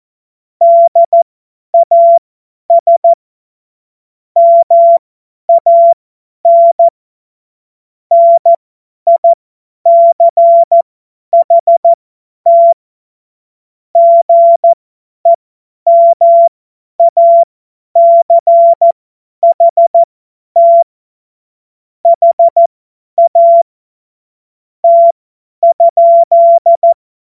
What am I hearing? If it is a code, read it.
Morse code, heard as a keyed tone, about 14 wpm, spaced as for 10 wpm, 680 Hz. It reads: DAS MAN NICHT GEMACHT HAT?
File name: Morsekurs / Witze / Witz 28. September 2021, 50 F70